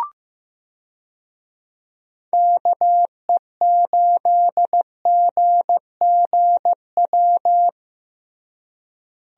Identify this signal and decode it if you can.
Morse code, heard as a keyed tone, about 15 wpm, 700 Hz. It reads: KE8GGW
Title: USA Call Signs - 6 Characters - Repeated 3x - 15wpm